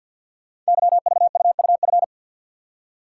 Morse code, heard as a keyed tone, about 52 wpm, 700 Hz. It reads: YVUUF
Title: Code Groups - 5 Letter:Numbers - 50wpm